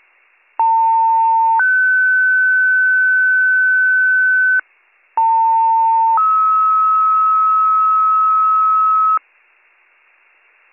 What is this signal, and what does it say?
Two-tone page, heard as a one-second tone, then a three-second tone, about 900 then 1500 Hz, about 900 then 1290 Hz. It is heard from 0.6 to 9.2 s.